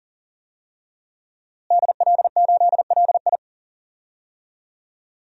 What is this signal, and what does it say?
Morse code, heard as a keyed tone, about 37 wpm, 700 Hz. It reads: DL8LI
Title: DXCC-100 Call Signs - 5 Characters - Repeated 3x - 40wpm